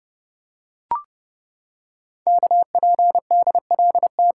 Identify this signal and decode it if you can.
Morse code, heard as a keyed tone, about 28 wpm, 700 Hz. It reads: KPDLT
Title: Code Groups - 5 Letters - 30wpm